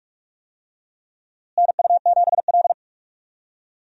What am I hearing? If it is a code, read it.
Morse code, heard as a keyed tone, about 41 wpm, 700 Hz. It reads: NU7L
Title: USA Call Signs - 4 Characters - Repeated 3x - 45wpm